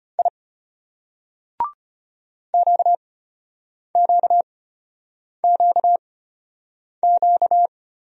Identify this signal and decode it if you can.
Morse code, heard as a keyed tone, about 33 wpm, spaced as for 8 wpm, 700 Hz. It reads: I QQQQ